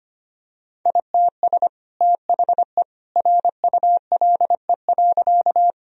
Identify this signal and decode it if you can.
Morse code, heard as a keyed tone, about 25 wpm, 700 Hz. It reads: ITS THE RULE.
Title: Sentences from Top 1000 Words - Review Part 2 - 25wpm